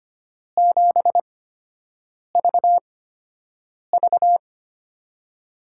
Morse code, heard as a keyed tone, about 25 wpm, spaced as for 7 wpm, 700 Hz. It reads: 7VV
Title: USA Call Signs - 5 Characters - 8x Character Spacing - 25wpm